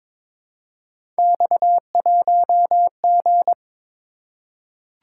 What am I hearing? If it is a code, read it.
Morse code, heard as a keyed tone, about 22 wpm, 700 Hz. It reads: X1G
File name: practice / 22wpm / Three Letter-Number Combinations - 22wpm